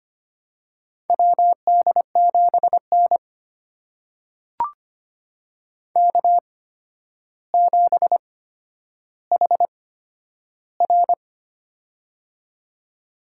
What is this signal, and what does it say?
Morse code, heard as a keyed tone, about 25 wpm, spaced as for 11 wpm, 700 Hz. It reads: WD7N K7HR
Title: USA Call Signs - 4 Characters - 8x Character Spacing - 25wpm